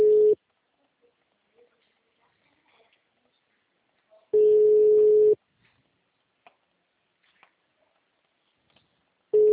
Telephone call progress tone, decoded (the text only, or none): ringback tone